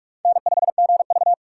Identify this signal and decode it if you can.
Morse code, heard as a keyed tone, about 47 wpm, 700 Hz. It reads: N5GV
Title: USA Call Signs - 4 Characters - 3x Character Spacing - 45wpm